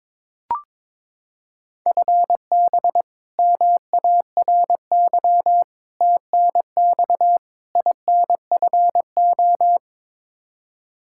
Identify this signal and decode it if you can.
Morse code, heard as a keyed tone, about 22 wpm, 700 Hz. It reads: FB MARY TNX INFO
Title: Practice QSO - Exchange 3 - More Info - 22wpm